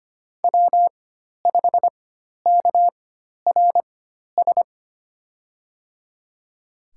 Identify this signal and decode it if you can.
Morse code, heard as a keyed tone, about 25 wpm, spaced as for 12 wpm, 700 Hz. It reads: W5KRS
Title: USA Call Signs - 5 Characters - 4x Character Spacing - 25wpm